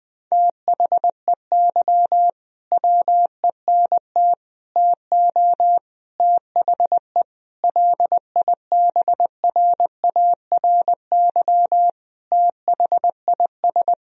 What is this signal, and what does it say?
Morse code, heard as a keyed tone, about 20 wpm, 700 Hz. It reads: THEY WENT TO THE LIBRARY THIS
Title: Sentences from Top 1000 Words - Review Part 4 - 20wpm